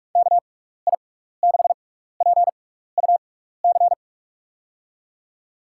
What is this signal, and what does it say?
Morse code, heard as a keyed tone, about 46 wpm, spaced as for 15 wpm, 700 Hz. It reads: KI6PUC
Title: USA Call Signs - 6 Characters - 6x Character Spacing - 45wpm